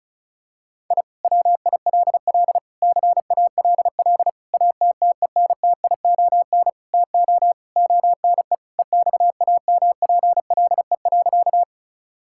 Morse code, heard as a keyed tone, about 36 wpm, 700 Hz. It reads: I WILL CALL ATTENTION TO ONE EXAMPLE.
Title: Sentences from Top 1000 Words - Review Part 6 - 35wpm